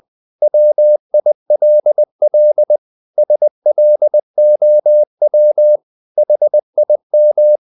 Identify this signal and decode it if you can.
Morse code, heard as a keyed tone, about 20 wpm, 600 Hz. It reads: WILL SLOW HIM
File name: the old man and the sea part 2 20wpm